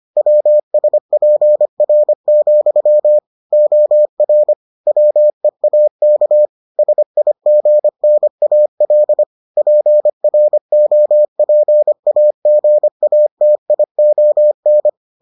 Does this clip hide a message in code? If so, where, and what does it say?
Yes: Morse code from 0.2 to 14.9 s, heard as a keyed tone, about 25 wpm, 600 Hz. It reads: WSPR, OR WEAK SIGNAL PROPAGATION